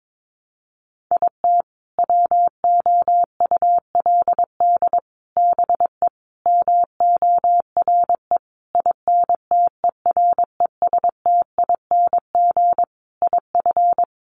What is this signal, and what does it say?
Morse code, heard as a keyed tone, about 22 wpm, 700 Hz. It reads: IT WOULD BE MORE INTERESTING IF
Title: Sentences from Top 1000 Words 22wpm